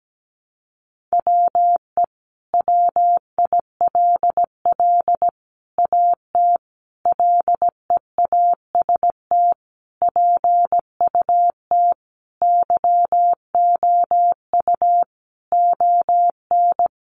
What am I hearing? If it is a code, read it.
Morse code, heard as a keyed tone, about 17 wpm, 700 Hz. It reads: WE WILL AT LEAST PUT YOU ON
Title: Sentences from Top 1200 Words 17wpm